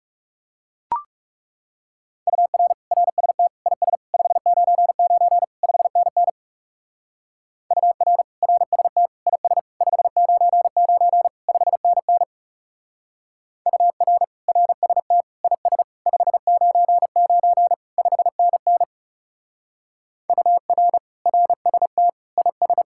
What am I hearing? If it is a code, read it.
Morse code, heard as a keyed tone, about 35 wpm, 700 Hz. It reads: UR RST IS 599 5NN UR RST IS 599 5NN UR RST IS 599 5NN UR RST IS
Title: Speed-Racing Common QSO Phrases - 30wpm